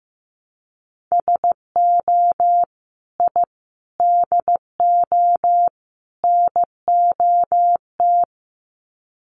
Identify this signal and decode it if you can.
Morse code, heard as a keyed tone, about 15 wpm, 700 Hz. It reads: SO I DO NOT
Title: Sets of 4 Words 15wpm